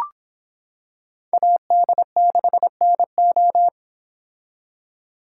Morse code, heard as a keyed tone, about 25 wpm, 700 Hz. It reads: AD6NO